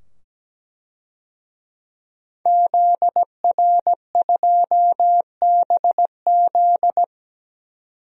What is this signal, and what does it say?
Morse code, heard as a keyed tone, about 17 wpm, 700 Hz. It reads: ZR2BZ
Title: DXCC-100 Call Signs - 5 Characters - 17wpm